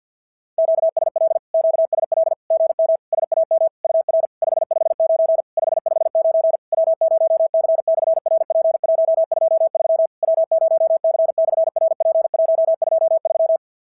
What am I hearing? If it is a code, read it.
Morse code, heard as a keyed tone, about 50 wpm, 690 Hz. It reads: QSL QSL GM SAM UR 559 559 W0C/RP123 W0C/RP123
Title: SOTA QSO Practice 50wpm